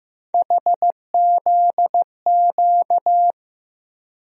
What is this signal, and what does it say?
Morse code, heard as a keyed tone, about 16 wpm, 700 Hz. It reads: HZQ
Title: USA Call Sign Postfixes - 3 Letters 15wpm